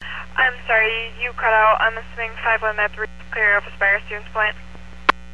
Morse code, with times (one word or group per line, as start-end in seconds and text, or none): none